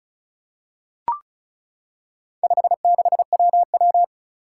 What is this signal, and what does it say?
Morse code, heard as a keyed tone, about 36 wpm, 700 Hz. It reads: 56WW
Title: Four Challenging Letter-Number Combinations 35wpm